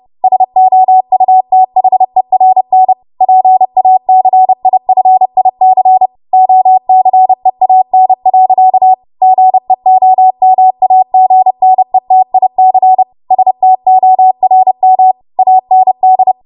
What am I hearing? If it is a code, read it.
Morse code, heard as a keyed tone, about 30 wpm, 750 Hz. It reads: SOUTHERN PACIFIC OCEAN. GEOMAGNETIC STORM AND
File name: Morse / 30WPM